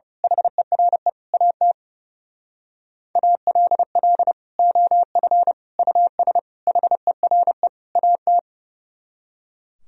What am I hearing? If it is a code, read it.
Morse code, heard as a keyed tone, about 30 wpm, 700 Hz. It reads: HERE AT ALL OF US HERE AT
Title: Speed-Racing - Sets of 5 Words 30wpm